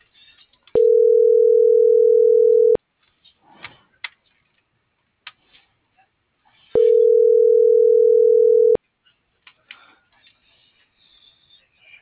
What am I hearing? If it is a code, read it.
Ringback tone (North America), 440 + 480 Hz.
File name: CENTRAL MULTIMODAL / out josimar.cruz - .7494